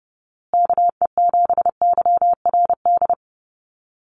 Morse code, heard as a keyed tone, about 30 wpm, 700 Hz. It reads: KE7YRD
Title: USA Call Signs Repeated 2x 30wpm